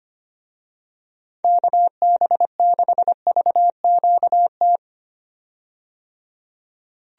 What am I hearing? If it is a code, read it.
Morse code, heard as a keyed tone, about 25 wpm, 700 Hz. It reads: KB6VQT
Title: USA Call Signs - 6 Characters - 25wpm